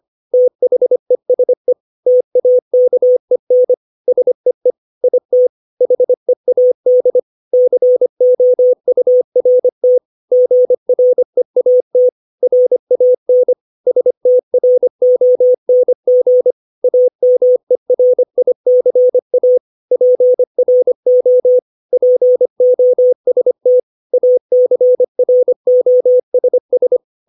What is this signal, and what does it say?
Morse code, heard as a keyed tone, about 25 wpm, 500 Hz. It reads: THESE TAKEN SEE IT HEAD COURT GREAT RAN STRONG AMERICA PRO POST ACROSS
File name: 632 words / 632 Words 25WPM